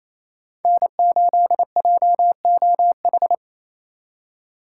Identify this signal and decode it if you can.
Morse code, heard as a keyed tone, about 36 wpm, 700 Hz. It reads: N8JOH